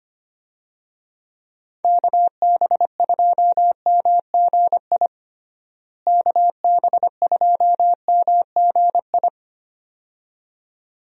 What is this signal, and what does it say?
Morse code, heard as a keyed tone, about 25 wpm, 700 Hz. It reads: KB2MGI KB2MGI
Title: USA Call Signs - 6 Characters - Repeated 2x - 25wpm